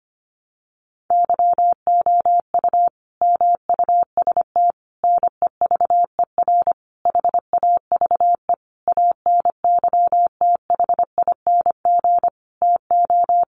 Morse code, heard as a keyed tone, about 25 wpm, 700 Hz. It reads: YOU MUST NEVER HAVE ANYTHING TO
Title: Sentences from Top 1000 Words 25wpm